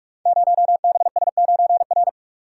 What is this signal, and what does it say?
Morse code, heard as a keyed tone, about 45 wpm, 690 Hz. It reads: 0BS9R
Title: Code Groups - 5 Letter:Numbers - 45wpm